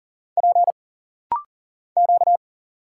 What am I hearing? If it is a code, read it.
Morse code, heard as a keyed tone, about 40 wpm, 700 Hz. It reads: P Q
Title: Single Letters - Rapid-Fire 40wpm